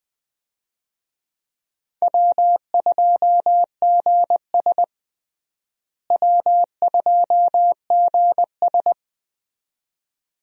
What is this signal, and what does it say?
Morse code, heard as a keyed tone, about 21 wpm, 700 Hz. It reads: W2GS W2GS
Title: USA Call Signs - 4 Characters - Repeated 2x - 20wpm